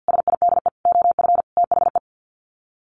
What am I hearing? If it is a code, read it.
Morse code, heard as a keyed tone, about 50 wpm, 700 Hz. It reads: SIDE OF THE
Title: Sets of 4 Words 50wpm